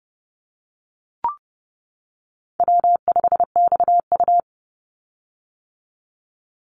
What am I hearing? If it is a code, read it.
Morse code, heard as a keyed tone, about 30 wpm, 700 Hz. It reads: W5XU
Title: Easy Call Signs - 30wpm